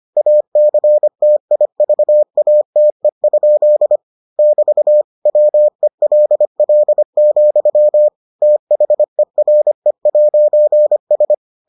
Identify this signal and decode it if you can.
Morse code, heard as a keyed tone, about 25 wpm, 600 Hz. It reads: ACTIVATE? <BT> WELL, THERE'S